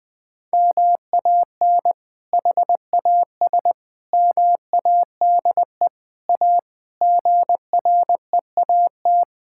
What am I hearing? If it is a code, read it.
Morse code, heard as a keyed tone, about 20 wpm, 700 Hz. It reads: MAN HAS MADE A GREAT
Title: Sentences from Top 1000 Words - Review Part 6 - 20wpm